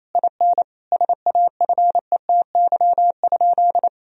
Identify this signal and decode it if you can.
Morse code, heard as a keyed tone, about 28 wpm, 700 Hz. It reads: IN SAFETY?
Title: Sentences from Top 1000 Words - Review Part 4 - 28wpm